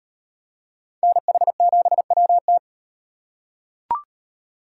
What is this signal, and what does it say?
Morse code, heard as a keyed tone, about 38 wpm, 700 Hz. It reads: NH7WT